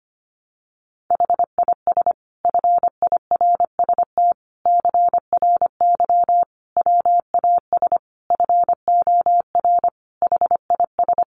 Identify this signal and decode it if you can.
Morse code, heard as a keyed tone, about 25 wpm, 700 Hz. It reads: HIS FIRST CRY WAS FOR HIS
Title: Sentences from Top 500 Words 25wpm